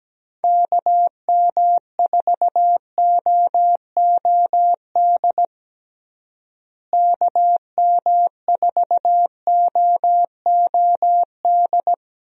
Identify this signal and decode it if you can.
Morse code, heard as a keyed tone, about 17 wpm, 700 Hz. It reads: KM4OOD KM4OOD
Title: USA Call Signs - 6 Characters - Repeated 3x - 17wpm